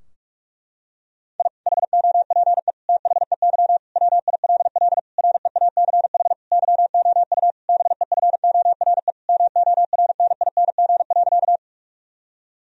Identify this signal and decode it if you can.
Morse code, heard as a keyed tone, about 47 wpm, 700 Hz. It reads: I HOPE THEY WILL REACH YOU BEFORE MORNING.
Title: Sentences from Top 1000 Words - Review Part 2 - 45wpm